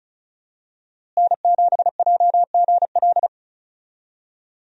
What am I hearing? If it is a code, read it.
Morse code, heard as a keyed tone, about 36 wpm, 700 Hz. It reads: N7JGL
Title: USA Call Signs - 5 Characters - Repeated 3x - 35wpm